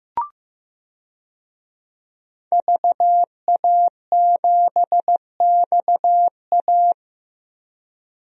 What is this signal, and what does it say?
Morse code, heard as a keyed tone, about 15 wpm, 700 Hz. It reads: VA7XA
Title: DXCC-100 Call Signs - 5 Characters - Repeated 3x - 15wpm